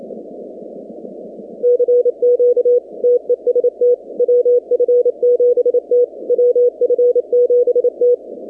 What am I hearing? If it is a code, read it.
Morse code, heard as a keyed tone, about 27 wpm, 490 Hz. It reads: CQ TEST WF7T WF7T
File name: 2019-cw-wpx-kt5j-k5pi